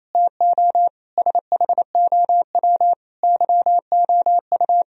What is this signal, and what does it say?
Morse code, heard as a keyed tone, about 34 wpm, 700 Hz. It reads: TO SHOW YOU
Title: Speed-Racing - Sets of 5 Words 28wpm